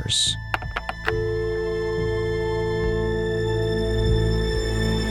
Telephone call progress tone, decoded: dial tone